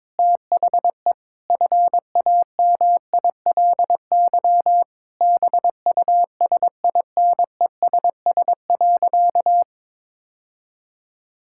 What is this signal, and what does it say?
Morse code, heard as a keyed tone, about 22 wpm, 700 Hz. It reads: THE FAMILY BUSINESS.
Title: Sentences from Top 1000 Words - Review Part 6 - 22wpm